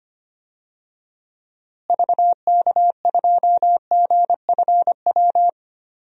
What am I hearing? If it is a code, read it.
Morse code, heard as a keyed tone, about 25 wpm, 700 Hz. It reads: VK2GFW
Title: DXCC-100 Call Signs - 6 Characters - Repeated 2x - 25wpm